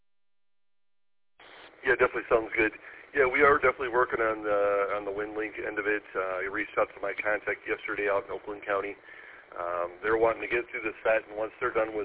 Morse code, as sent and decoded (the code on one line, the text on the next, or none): none